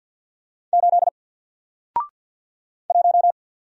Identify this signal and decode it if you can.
Morse code, heard as a keyed tone, about 50 wpm, 700 Hz. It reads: 8 1
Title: Single Letter-Number Alt 50wpm